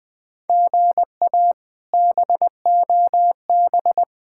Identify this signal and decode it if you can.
Morse code, heard as a keyed tone, about 21 wpm, 700 Hz. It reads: GA BOB
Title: K1USN SST QSO Practice - 20wpm